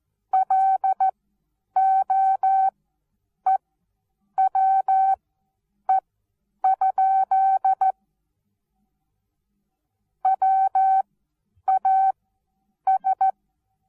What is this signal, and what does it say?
Morse code, heard as a keyed tone, about 14 wpm, spaced as for 8 wpm, 770 Hz. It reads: LOEWE? WAS
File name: Onlinelehrgang / morsen / DJ4UF Morsen19